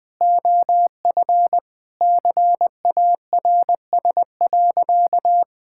Morse code, heard as a keyed tone, about 20 wpm, 700 Hz. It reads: OF CARS.
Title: Sentences with 5-Word Phrases - 20wpm